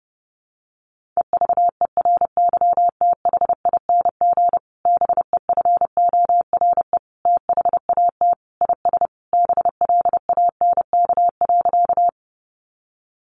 Morse code, heard as a keyed tone, about 30 wpm, 700 Hz. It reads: EVERYTHING BEFORE THAT IS BLANK.
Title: Sentences from Top 1500 Words 30wpm